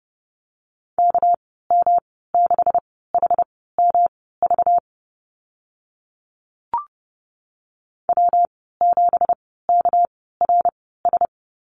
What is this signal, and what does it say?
Morse code, heard as a keyed tone, about 32 wpm, spaced as for 17 wpm, 700 Hz. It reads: KM6HMV W7KRS
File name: Lesson 37g - Letter x - Call Signs 30wpm